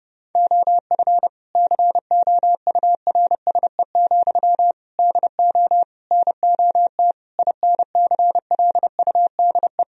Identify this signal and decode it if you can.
Morse code, heard as a keyed tone, about 32 wpm, 700 Hz. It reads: OF COURSE, DO NOT INCLUDE
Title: Sentences from Top 1000 Words - Review Part 3 - 30wpm